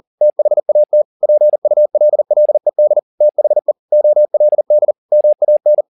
Morse code, heard as a keyed tone, about 40 wpm, 600 Hz. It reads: THAT PULLED THE OLD MAN